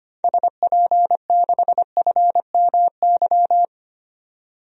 Morse code, heard as a keyed tone, about 25 wpm, 700 Hz. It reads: SP6FMY
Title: DXCC-100 Call Signs - 6 Characters - 25wpm